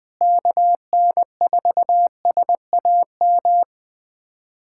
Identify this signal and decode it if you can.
Morse code, heard as a keyed tone, about 20 wpm, 700 Hz. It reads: KN4SAM
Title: USA Callsign DE Callsign 20wpm